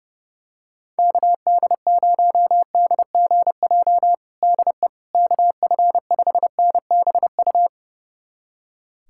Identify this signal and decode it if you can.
Morse code, heard as a keyed tone, about 30 wpm, 700 Hz. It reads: KD0DGJ DE KF5NBU
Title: USA Callsign DE Callsign - 6 Characters - 30wpm